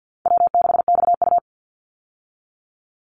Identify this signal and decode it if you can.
Morse code, heard as a keyed tone, about 50 wpm, 700 Hz. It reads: W6XU
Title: Easy Call Signs Repeated 3x - 50wpm